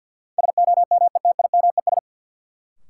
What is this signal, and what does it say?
Morse code, heard as a keyed tone, about 51 wpm, 700 Hz. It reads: SOMETIMES
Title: Sentences from Top 600 Words - New Word Drill - 50wpm